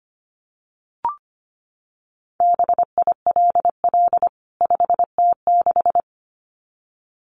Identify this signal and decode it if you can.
Morse code, heard as a keyed tone, about 25 wpm, 700 Hz. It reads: BILL 5T6
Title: CWT Exchange 25wpm